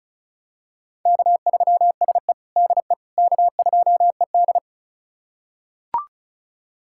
Morse code, heard as a keyed tone, about 36 wpm, 700 Hz. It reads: K3SE DE K2ED
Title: USA Callsign DE Callsign - 4 Characters - 35wpm